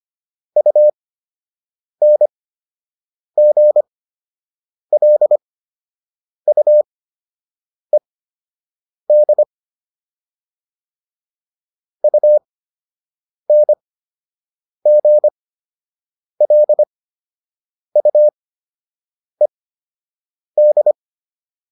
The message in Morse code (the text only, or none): UNGLUED UNGLUED